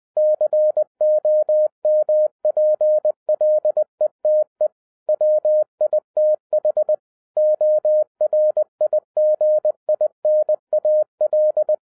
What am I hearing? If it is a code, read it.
Morse code, heard as a keyed tone, about 20 wpm, 610 Hz. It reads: COMPLETE WITH ORIGINAL
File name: 2011-07-mp3 / n43 Czech Army Morse Keys 20wpm